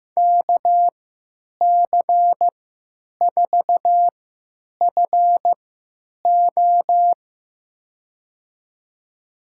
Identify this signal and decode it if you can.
Morse code, heard as a keyed tone, about 16 wpm, spaced as for 9 wpm, 700 Hz. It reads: KC4FO
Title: USA Call Signs - 5 Characters - 3x Character Spacing - 15wpm